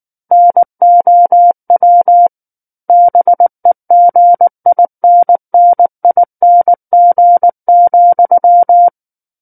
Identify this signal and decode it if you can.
Morse code, heard as a keyed tone, about 19 wpm, 700 Hz. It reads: NOW BEGINNING,
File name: Book One / 10-Chapter 10 - 19 WPM